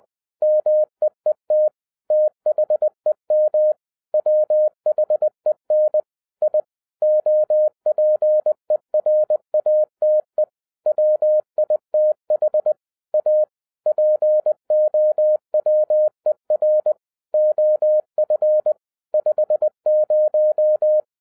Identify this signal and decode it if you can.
Morse code, heard as a keyed tone, about 20 wpm, 610 Hz. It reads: MEET THEM WHEN I OPERATE WITH A POWER OF 50
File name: 2011-10-mp3 / n13-TIPS FROM MY 34 YEARS OF EXPERIENCE 20wpm